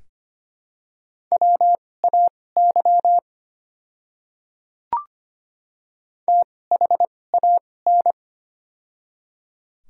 Morse code, heard as a keyed tone, about 25 wpm, spaced as for 18 wpm, 700 Hz. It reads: WAY THAN
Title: 2x Character Spacing - Top 100 Words - 25wpm